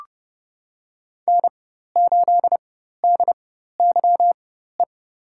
Morse code, heard as a keyed tone, about 32 wpm, spaced as for 14 wpm, 700 Hz. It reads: N8DYE
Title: USA Call Signs - 5 Characters - 4x Character Spacing - 30wpm